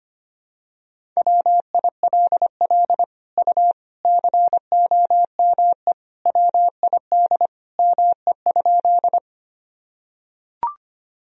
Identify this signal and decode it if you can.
Morse code, heard as a keyed tone, about 25 wpm, 700 Hz. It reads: WILL U COME WID ME?
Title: Sentences from Top 200 Words and Abbreviations 25wpm